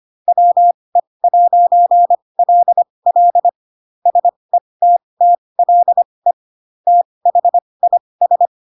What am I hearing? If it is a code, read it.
Morse code, heard as a keyed tone, about 25 wpm, spaced as for 20 wpm, 700 Hz. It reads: WE'LL SETTLE THIS